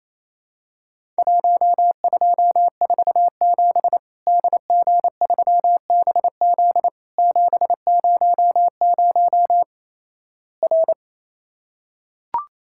Morse code, heard as a keyed tone, about 28 wpm, 700 Hz. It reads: 1247 DG3BZ 700 R
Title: Worked All Europe DX Contest - QTC Exchange - 28wpm